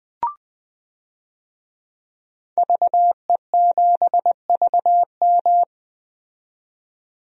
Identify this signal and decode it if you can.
Morse code, heard as a keyed tone, about 20 wpm, 700 Hz. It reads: VE7VM
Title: DXCC-100 Call Signs - 5 Characters - 20wpm